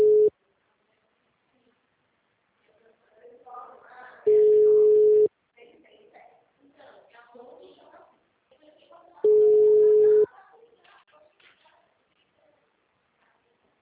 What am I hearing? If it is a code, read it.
Ringback tone (Europe, 425 Hz), with speech.